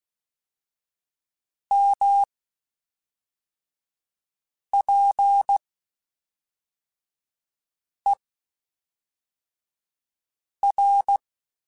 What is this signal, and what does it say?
Morse code, heard as a keyed tone, about 16 wpm, spaced as for 3 wpm, 800 Hz. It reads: MPER
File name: Quotes Spaced / Quotes Spaced-mp3s / Temper sp @5